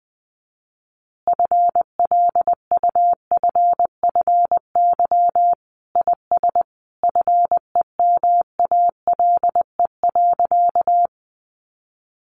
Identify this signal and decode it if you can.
Morse code, heard as a keyed tone, about 20 wpm, 700 Hz. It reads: FLUFFY IS FEMALE.
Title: Easy Phrases 20wpm